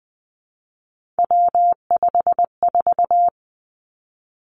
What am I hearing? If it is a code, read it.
Morse code, heard as a keyed tone, about 20 wpm, 700 Hz. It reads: W54
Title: Three Challenging Letter-Number Combinations 20wpm